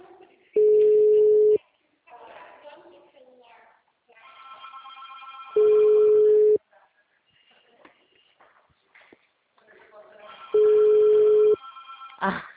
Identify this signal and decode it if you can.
Ringback tone (Europe, 425 Hz), with speech.
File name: out- -2031- - - .37931